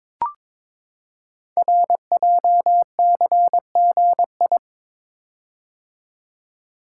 Morse code, heard as a keyed tone, about 23 wpm, 700 Hz. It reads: RJCGI
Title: Code Groups - 5 Letters - 22wpm